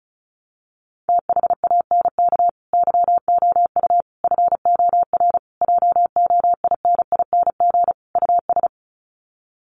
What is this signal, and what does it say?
Morse code, heard as a keyed tone, about 34 wpm, 700 Hz. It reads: THANK YOU FOR JOINING US
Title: Sets of 5 Words 35wpm